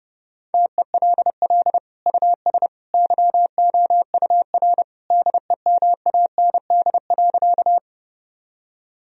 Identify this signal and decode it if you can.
Morse code, heard as a keyed tone, about 30 wpm, 700 Hz. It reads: TELL US YOUR DEMAND.
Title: Sentences from Top 1000 Words - Review Part 4 - 30wpm